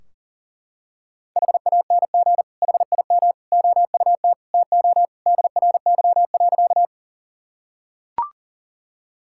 Morse code, heard as a keyed tone, about 40 wpm, 700 Hz. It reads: HANG HIM OUT TO DRY.
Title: Sentences from Top 1600 Words 40wpm